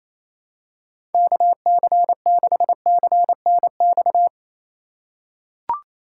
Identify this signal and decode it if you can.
Morse code, heard as a keyed tone, about 28 wpm, 700 Hz. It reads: KC6CNX
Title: USA Call Signs - 6 Characters - 3x Character Spacing - 28wpm